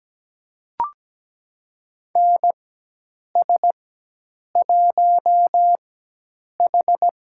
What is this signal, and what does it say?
Morse code, heard as a keyed tone, about 17 wpm, spaced as for 8 wpm, 700 Hz. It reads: NS1H
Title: USA Call Signs - 4 Characters - 4x Character Spacing - 17wpm